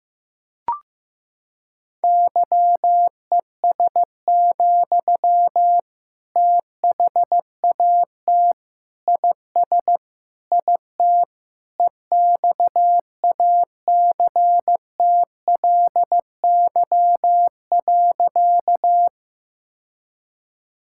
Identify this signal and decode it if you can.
Morse code, heard as a keyed tone, about 14 wpm, 700 Hz. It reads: YES, THAT IS IT EXACTLY.
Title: Sentences from Top 1000 Words - Review Part 6 - 15wpm